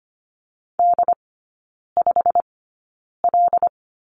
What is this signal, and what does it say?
Morse code, heard as a keyed tone, about 25 wpm, spaced as for 9 wpm, 700 Hz. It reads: D5L
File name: Lesson 15e - Number 5 - Three Characters 25wpm